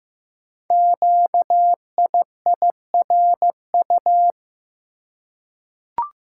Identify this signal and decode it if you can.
Morse code, heard as a keyed tone, about 15 wpm, 690 Hz. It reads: QIIRU
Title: Code Groups - 5 Letters - 15wpm